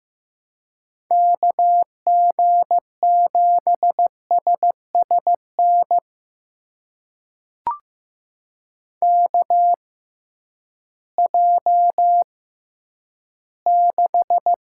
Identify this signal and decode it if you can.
Morse code, heard as a keyed tone, about 16 wpm, spaced as for 10 wpm, 700 Hz. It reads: KG7SSN KJ6
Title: USA Call Signs - 6 Characters - 6x Character Spacing - 15wpm